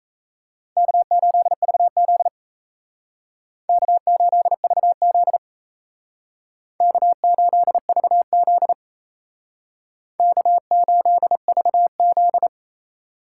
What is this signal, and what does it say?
Morse code, heard as a keyed tone, about 37 wpm, 700 Hz. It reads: K8VZ K8VZ K8VZ K8VZ